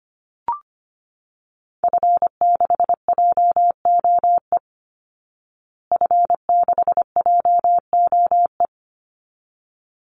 Morse code, heard as a keyed tone, about 25 wpm, 700 Hz. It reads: F6JOE F6JOE
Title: Call Signs Repeated 3x - 25wpm